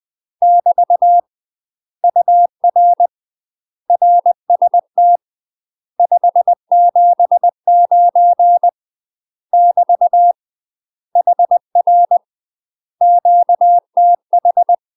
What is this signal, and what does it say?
Morse code, heard as a keyed tone, about 20 wpm, 700 Hz. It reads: <BT> UR RST 579 <BT> HR QTH
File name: UK MASTER QSOs TRAINING / 20WPM / UK MASTER QSOs 0056